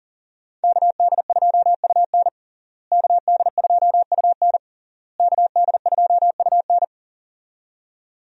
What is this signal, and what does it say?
Morse code, heard as a keyed tone, about 40 wpm, 700 Hz. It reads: KD2UN KD2UN KD2UN
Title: USA Call Signs - 5 Characters - Repeated 3x - 40wpm